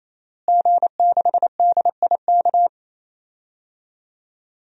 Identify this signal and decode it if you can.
Morse code, heard as a keyed tone, about 27 wpm, 700 Hz. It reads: G6DIK
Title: DXCC-100 Call Signs - 5 Characters - Repeated 3x - 28wpm